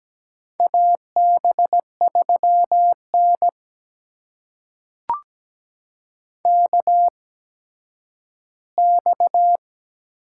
Morse code, heard as a keyed tone, about 17 wpm, spaced as for 10 wpm, 700 Hz. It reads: AB3N KX
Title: USA Call Signs - 4 Characters - 8x Character Spacing - 17wpm